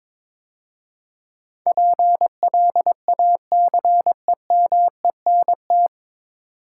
Morse code, heard as a keyed tone, about 22 wpm, 700 Hz. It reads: PLACEMENT
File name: 9 Letter Words - First 3 Letters Given - 22wpm